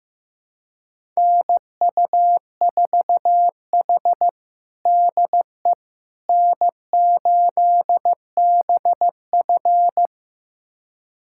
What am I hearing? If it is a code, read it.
Morse code, heard as a keyed tone, about 16 wpm, 700 Hz. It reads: NU4H DE N8BF
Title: USA Callsign DE Callsign - 4 Characters - 15wpm